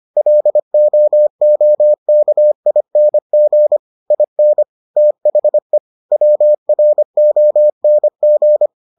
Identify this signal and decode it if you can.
Morse code, heard as a keyed tone, about 25 wpm, 600 Hz. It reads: LOOKING IN THE WRONG